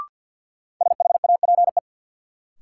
Morse code, heard as a keyed tone, about 50 wpm, 700 Hz. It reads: SHAPE
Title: Sentences from Top 500 Words - New Word Drill - 50wpm